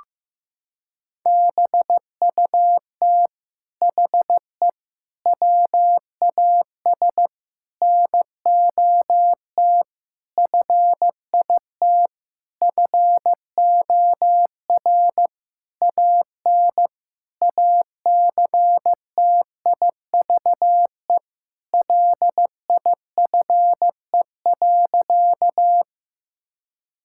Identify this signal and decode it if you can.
Morse code, heard as a keyed tone, about 15 wpm, 700 Hz. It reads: BUT HE WAS NOT FIT FOR AN ACTIVE LIFE.
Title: Sentences from Top 1000 Words - Review Part 6 - 15wpm